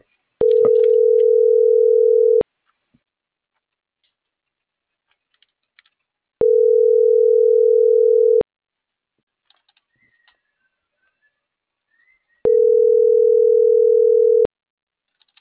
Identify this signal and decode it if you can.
Ringback tone (North America), 440 + 480 Hz.